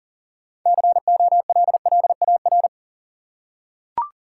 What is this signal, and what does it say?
Morse code, heard as a keyed tone, about 40 wpm, 700 Hz. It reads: COLLAR
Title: Six Letter Words - 40wpm